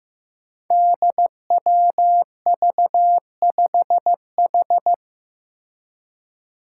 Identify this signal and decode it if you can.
Morse code, heard as a keyed tone, about 15 wpm, 700 Hz. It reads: DWV5H
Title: Five Challenging Letter-Number Combinations 15wpm